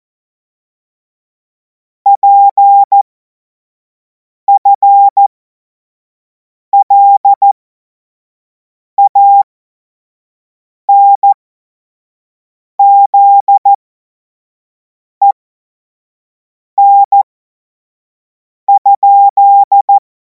Morse code, heard as a keyed tone, about 14 wpm, spaced as for 5 wpm, 800 Hz. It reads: PFLANZEN?
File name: Morsekurs / Witze / Witz 03. März 2026 25 F70